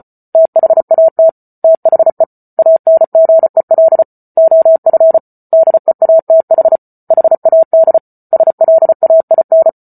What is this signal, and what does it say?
Morse code, heard as a keyed tone, about 34 wpm, 650 Hz. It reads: THAT THE ANGEL OF DEATH HAD SLAIN